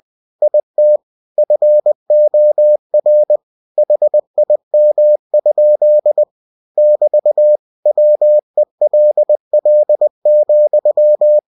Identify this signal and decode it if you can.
Morse code, heard as a keyed tone, about 20 wpm, 600 Hz. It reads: IT FOR HIM? <BT> WELL,
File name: cw-books / winnie-the-pooh / winnie-20wpm-page-0029